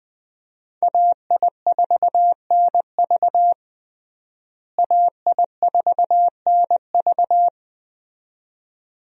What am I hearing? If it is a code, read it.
Morse code, heard as a keyed tone, about 20 wpm, 700 Hz. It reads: AI4NV AI4NV
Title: USA Call Signs - 5 Characters - Repeated 2x - 20wpm